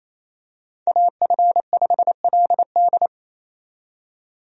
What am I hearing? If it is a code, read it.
Morse code, heard as a keyed tone, about 28 wpm, 700 Hz. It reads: AF5LD
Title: USA Call Signs - 5 Characters - 28wpm